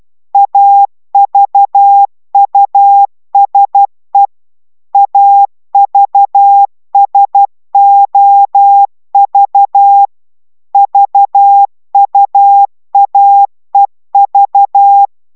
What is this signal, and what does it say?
Morse code, heard as a keyed tone, about 12 wpm, 800 Hz. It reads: AVUSE AVSOV VUAEV